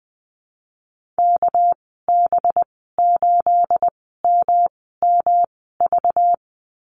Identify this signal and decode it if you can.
Morse code, heard as a keyed tone, about 20 wpm, spaced as for 15 wpm, 700 Hz. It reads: KB8MMV
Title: 231 Lesson 37g - Letter x - Call Signs 20wpm